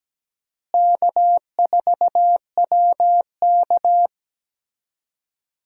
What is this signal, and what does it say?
Morse code, heard as a keyed tone, about 17 wpm, 700 Hz. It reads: K4WK
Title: USA Call Signs - 4 Characters - Repeated 2x - 17wpm